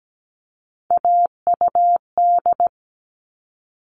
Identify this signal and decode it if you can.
Morse code, heard as a keyed tone, about 17 wpm, 700 Hz. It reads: AUD
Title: All Cut Numbers - 3-Digit 17wpm